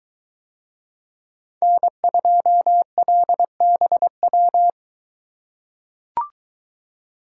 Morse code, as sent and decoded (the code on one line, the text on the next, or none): -. ..--- .-.. -... .--
N2LBW